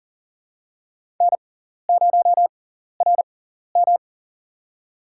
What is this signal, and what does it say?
Morse code, heard as a keyed tone, about 40 wpm, spaced as for 14 wpm, 700 Hz. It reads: N0RM
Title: USA Call Signs - 6x Character Spacing - 40wpm